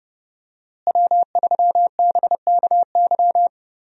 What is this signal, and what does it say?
Morse code, heard as a keyed tone, about 30 wpm, 700 Hz. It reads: W3BKY
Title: USA Call Signs - 5 Characters - Repeated 2x - 30wpm